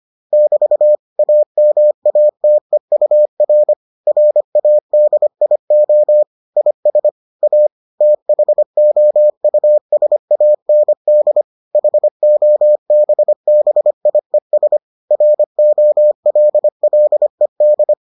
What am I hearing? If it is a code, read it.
Morse code, heard as a keyed tone, about 25 wpm, 600 Hz. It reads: <BT> AMATEUR RADIO IS A THOUSAND HOBBIES ROLLED